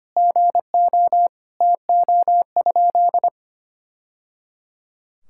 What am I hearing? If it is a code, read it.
Morse code, heard as a keyed tone, about 25 wpm, 700 Hz. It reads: GO TO?
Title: Sentences from Top 1000 Words - Review Part 5 - 25wpm